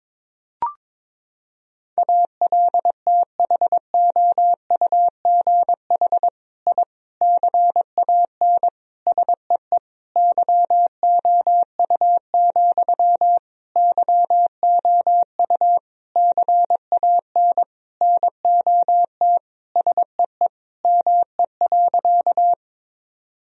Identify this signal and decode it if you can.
Morse code, heard as a keyed tone, about 22 wpm, 700 Hz. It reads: ALTHOUGH I CAN SEE YOU, YOU CAN NOT SEE ME.
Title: Sentences from Top 1000 Words - Review Part 3 - 22wpm